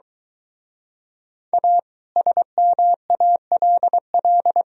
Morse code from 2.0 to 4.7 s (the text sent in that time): SMALL